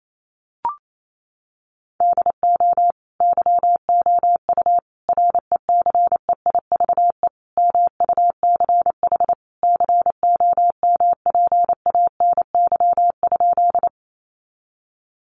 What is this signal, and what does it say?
Morse code, heard as a keyed tone, about 28 wpm, 700 Hz. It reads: DO YOU RECEIVE MUCH COMPANY?
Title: Sentences from Top 1000 Words 28wpm